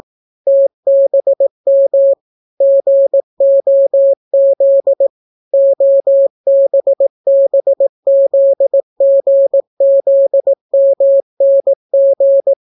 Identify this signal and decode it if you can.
Morse code, heard as a keyed tone, about 18 wpm, 550 Hz. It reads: TBM GOZ OBBZGZMNG